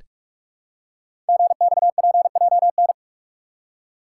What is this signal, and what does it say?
Morse code, heard as a keyed tone, about 44 wpm, 700 Hz. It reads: GXPJN
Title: Code Groups - 5 Letter:Numbers - 45wpm